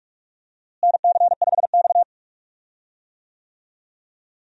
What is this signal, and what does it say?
Morse code, heard as a keyed tone, about 46 wpm, 700 Hz. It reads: NC5X
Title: USA Call Signs - 4 Characters - Repeated 3x - 45wpm